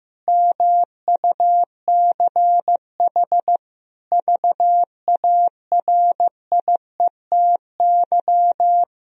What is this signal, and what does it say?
Morse code, heard as a keyed tone, about 15 wpm, 700 Hz. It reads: MUCH VARIETY
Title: Sentences from Top 1000 Words - Review Part 4 - 15wpm